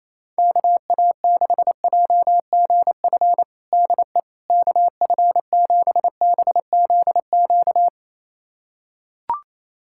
Morse code, heard as a keyed tone, about 28 wpm, 700 Hz. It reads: KA6JGF DE KF7BZQ
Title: USA Callsign DE Callsign - 6 Characters - 28wpm